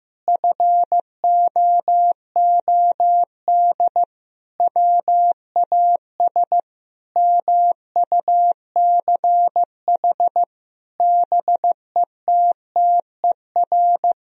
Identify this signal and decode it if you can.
Morse code, heard as a keyed tone, about 15 wpm, 700 Hz. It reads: FOOD WAS MUCH BETTER
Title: Sentences from Top 1000 Words - Review Part 2 - 15wpm